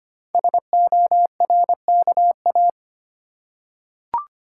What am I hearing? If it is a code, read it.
Morse code, heard as a keyed tone, about 25 wpm, 700 Hz. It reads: SORKA
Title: Code Groups - 5 Letters - 25wpm